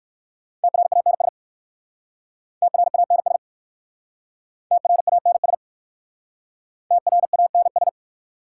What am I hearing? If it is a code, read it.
Morse code, heard as a keyed tone, about 60 wpm, 700 Hz. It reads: TRANS TRANS TRANS TRANS